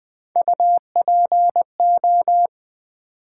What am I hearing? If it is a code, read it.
Morse code, heard as a keyed tone, about 20 wpm, 700 Hz. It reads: UPO
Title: Build A Word - 1 Letter At A Time - 4-Letter Words - 20wpm